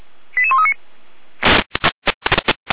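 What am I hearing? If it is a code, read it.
Five-tone selective call (ZVEI), heard as quick runs of five tones, tones of about 70 ms. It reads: 88139
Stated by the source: ZVEI five-tone selective call received on an SDR.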